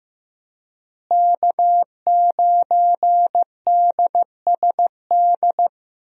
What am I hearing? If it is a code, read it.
Morse code, heard as a keyed tone, about 15 wpm, 700 Hz. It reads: K9DSD
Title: USA Call Signs - 5 Characters - Repeated 3x - 15wpm